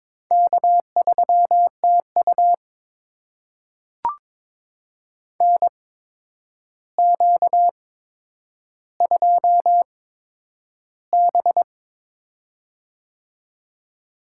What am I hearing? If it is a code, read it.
Morse code, heard as a keyed tone, about 22 wpm, spaced as for 9 wpm, 700 Hz. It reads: K3TU NQ2B
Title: USA Call Signs - 8x Character Spacing - 22wpm